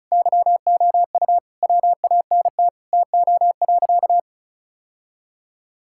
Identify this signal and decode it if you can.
Morse code, heard as a keyed tone, about 36 wpm, 700 Hz. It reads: YOU WANT TO.
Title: Sentences from Top 1600 Words 35wpm